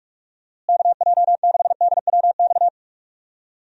Morse code, heard as a keyed tone, about 46 wpm, 700 Hz. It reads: KJ6DWX
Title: USA Call Signs - 6 Characters - Repeated 3x - 45wpm